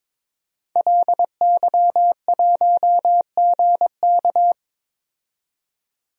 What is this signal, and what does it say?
Morse code, heard as a keyed tone, about 22 wpm, 700 Hz. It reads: LY1GK
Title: DXCC-100 Call Signs - 5 Characters - 22wpm